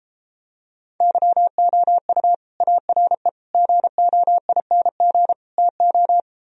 Morse code, heard as a keyed tone, about 29 wpm, 700 Hz. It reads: YOU ARE GOING TO